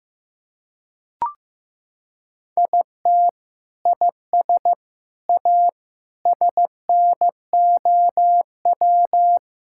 Morse code, heard as a keyed tone, about 15 wpm, 700 Hz. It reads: IT IS A SNOW
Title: Sentences from Top 1600 Words 15wpm